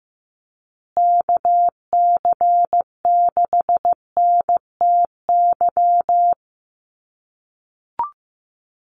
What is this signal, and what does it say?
Morse code, heard as a keyed tone, about 15 wpm, 700 Hz. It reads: KC6NTY
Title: USA Call Signs Repeated 2x 15wpm